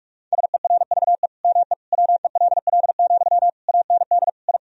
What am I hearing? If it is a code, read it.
Morse code, heard as a keyed tone, about 46 wpm, 700 Hz. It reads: SERVE ME WELL, AND I
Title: Sentences from Top 1000 Words - Review Part 2 - 45wpm